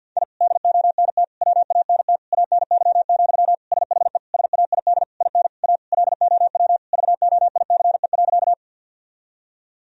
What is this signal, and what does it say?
Morse code, heard as a keyed tone, about 49 wpm, 700 Hz. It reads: I DONT WANT ANY, SHE SAID IN A LOW VOICE.
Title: Sentences from Top 1000 Words - Review Part 1 - 50wpm